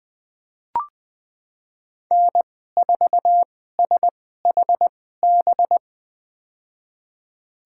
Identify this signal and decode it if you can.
Morse code, heard as a keyed tone, about 21 wpm, spaced as for 15 wpm, 700 Hz. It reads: N4SHB
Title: USA Call Signs - 5 Characters - 2x Character Spacing - 20wpm